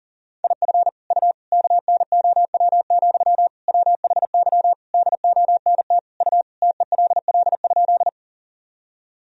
Morse code, heard as a keyed tone, about 40 wpm, 700 Hz. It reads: IF U KNOW, WHY DONT U TELL?
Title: Sentences from Top 300 Words and Abbreviations 40wpm